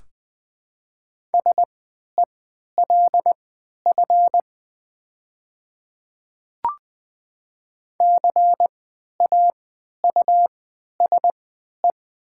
Morse code, heard as a keyed tone, about 20 wpm, spaced as for 11 wpm, 700 Hz. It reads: SELF CAUSE
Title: 3x Character Spacing - Top 300 Words - 20wpm